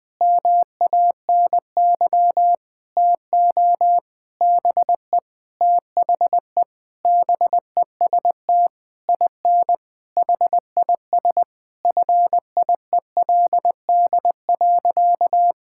Morse code, heard as a keyed tone, about 20 wpm, 700 Hz. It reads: MANY TO BE THE BEST IN HIS FIELD.
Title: Sentences with 7-Word Phrases - 20wpm